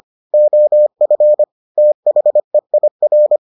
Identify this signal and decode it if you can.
Morse code, heard as a keyed tone, about 25 wpm, 600 Hz. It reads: OF THEIR